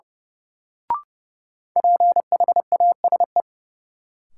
Morse code, heard as a keyed tone, about 30 wpm, 700 Hz. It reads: PHASE
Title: Sentences from Top 1200 Words - New Word Drill Alt - 30wpm